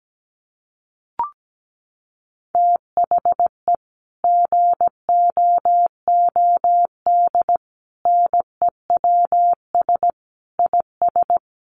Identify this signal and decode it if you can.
Morse code, heard as a keyed tone, about 17 wpm, 700 Hz. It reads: THE GOOD NEWS IS
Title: Sets of 5 Words 17wpm